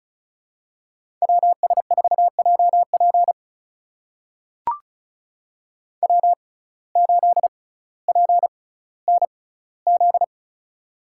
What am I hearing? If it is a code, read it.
Morse code, heard as a keyed tone, about 36 wpm, spaced as for 18 wpm, 700 Hz. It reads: WS4JP W8PNZ
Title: USA Call Signs - 5 Characters - 6x Character Spacing - 35wpm